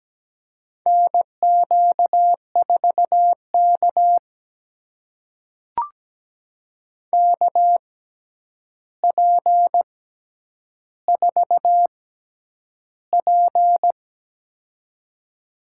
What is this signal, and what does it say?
Morse code, heard as a keyed tone, about 17 wpm, spaced as for 9 wpm, 700 Hz. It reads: NQ4K KP4P
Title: USA Call Signs - 4 Characters - 6x Character Spacing - 17wpm